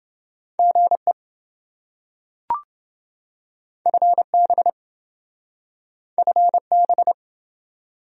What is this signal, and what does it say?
Morse code, heard as a keyed tone, about 27 wpm, 700 Hz. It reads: GE FB FB